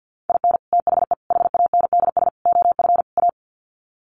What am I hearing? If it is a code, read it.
Morse code, heard as a keyed tone, about 48 wpm, 700 Hz. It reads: IN THE HANDS OF A
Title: Sets of 5 Words Encore 50wpm